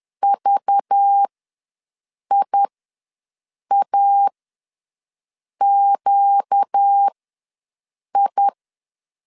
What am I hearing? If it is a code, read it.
Morse code, heard as a keyed tone, about 10 wpm, spaced as for 6 wpm, 800 Hz. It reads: VIAQI
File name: LEZIONI / 5 LBVC / 5 LBVC ws 10wpm 3733